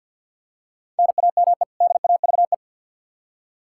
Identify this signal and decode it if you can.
Morse code, heard as a keyed tone, about 46 wpm, 700 Hz. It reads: NAME DAVE
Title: Practice QSO - Exchange 1 - Beginning - 50wpm